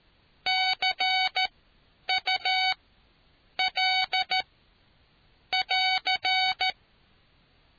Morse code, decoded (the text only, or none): CUL <AR>